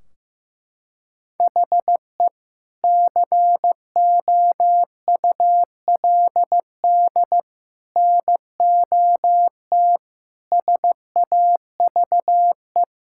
Morse code, heard as a keyed tone, about 15 wpm, 700 Hz. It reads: HE COULD NOT SAVE
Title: Sentences from Top 1600 Words 15wpm